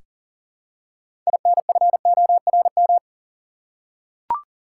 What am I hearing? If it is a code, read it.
Morse code, heard as a keyed tone, about 40 wpm, 700 Hz. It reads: INFORM
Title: Six Letter Words - 40wpm